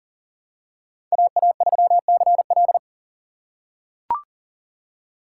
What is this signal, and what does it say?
Morse code, heard as a keyed tone, about 40 wpm, 700 Hz. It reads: AA3CL
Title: USA Call Signs - 3x Character Spacing - 40wpm